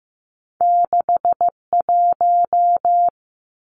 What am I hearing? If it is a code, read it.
Morse code, heard as a keyed tone, about 15 wpm, 700 Hz. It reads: 61
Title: Numbers - 2-Digit 15wpm